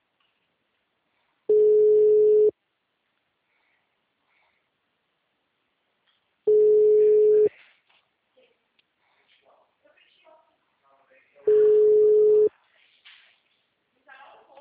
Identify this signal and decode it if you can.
Ringback tone (Europe, 425 Hz), with speech.